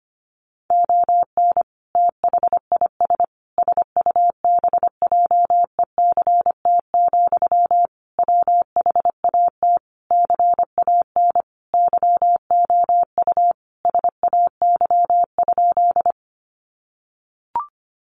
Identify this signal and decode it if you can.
Morse code, heard as a keyed tone, about 25 wpm, 700 Hz. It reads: ON THIS SUBJECT, WHAT CAN YOU SAY?
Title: Sentences from Top 800 Words 25wpm